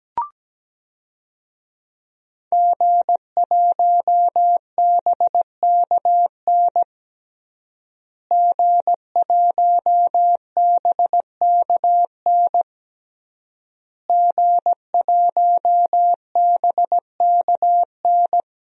Morse code, heard as a keyed tone, about 17 wpm, 700 Hz. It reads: G1BKN G1BKN G1BKN
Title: DXCC-100 Call Signs - 5 Characters - Repeated 3x - 17wpm